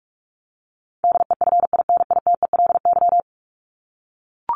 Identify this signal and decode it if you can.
Morse code, heard as a keyed tone, about 44 wpm, 700 Hz. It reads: DEFINITELY
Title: Top 1751-2000 Words 45wpm